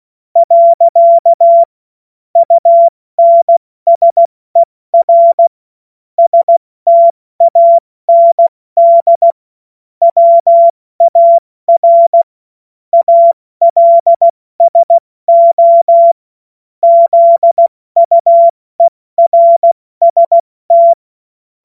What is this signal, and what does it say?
Morse code, heard as a keyed tone, about 16 wpm, 680 Hz. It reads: . UNSER STAND WAR ALSO ZUERST
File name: Morsekurs / Witze / Witz 21.November 2023 70 F80